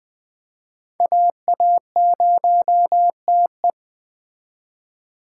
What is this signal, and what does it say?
Morse code, heard as a keyed tone, about 20 wpm, 700 Hz. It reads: AA0TE